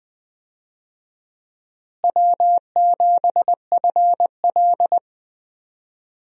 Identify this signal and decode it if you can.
Morse code, heard as a keyed tone, about 20 wpm, 700 Hz. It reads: W7FL
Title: USA Call Signs - 4 Characters - Repeated 2x - 20wpm